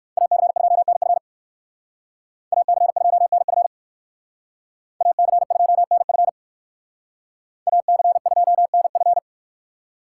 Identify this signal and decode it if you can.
Morse code, heard as a keyed tone, about 60 wpm, 700 Hz. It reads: AC2NF AC2NF AC2NF AC2NF